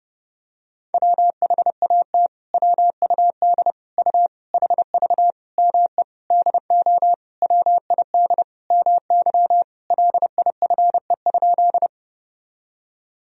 Morse code, heard as a keyed tone, about 32 wpm, 700 Hz. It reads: WHAT WUD U HV ME DO WID MY LIFE?
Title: Sentences from Top 300 Words and Abbreviations 30wpm